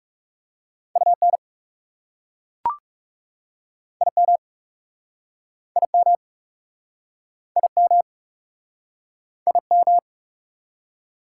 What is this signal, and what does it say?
Morse code, heard as a keyed tone, about 43 wpm, 690 Hz. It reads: UN IM IM IM IM